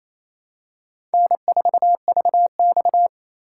Morse code, heard as a keyed tone, about 28 wpm, 700 Hz. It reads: N4VX
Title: USA Call Signs - 4 Characters - 2x Character Spacing - 28wpm